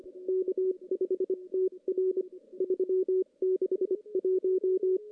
Morse code, heard as a keyed tone, about 25 wpm, 380 Hz. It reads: K5TR 361
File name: CW / CW-SS-k5tr- -1700